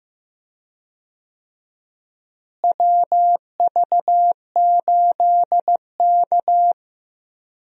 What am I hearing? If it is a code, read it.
Morse code, heard as a keyed tone, about 15 wpm, 700 Hz. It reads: WV8K
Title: USA Call Signs - 4 Characters - Repeated 3x - 15wpm